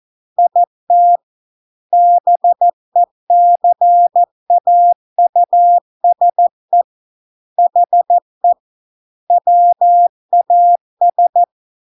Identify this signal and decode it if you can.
Morse code, heard as a keyed tone, about 14 wpm, 700 Hz. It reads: IT BECAUSE HE WAS